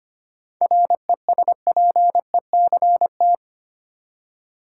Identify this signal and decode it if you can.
Morse code, heard as a keyed tone, about 25 wpm, 700 Hz. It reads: RESPECT
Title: Seven Letter Words - 25wpm